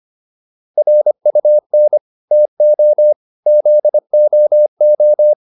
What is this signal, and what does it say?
Morse code, heard as a keyed tone, about 25 wpm, 600 Hz. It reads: RUN TO ZOO